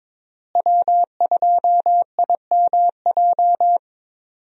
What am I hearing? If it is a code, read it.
Morse code, heard as a keyed tone, about 21 wpm, 700 Hz. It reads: W2IMJ
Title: USA Call Signs - 5 Characters - Repeated 3x - 22wpm